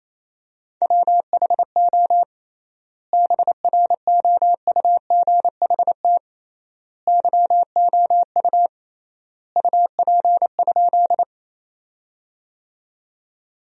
Morse code, heard as a keyed tone, about 28 wpm, 700 Hz. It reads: WHO BROUGHT YOU UP?
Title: 3x Word-Spacing - Sentences from Top 500 Words - 28wpm